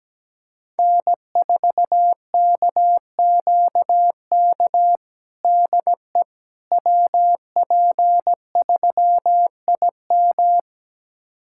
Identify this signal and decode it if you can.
Morse code, heard as a keyed tone, about 17 wpm, 700 Hz. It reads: N4KQK DE WP3IM
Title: USA Callsign DE Callsign - 5 Characters - 17wpm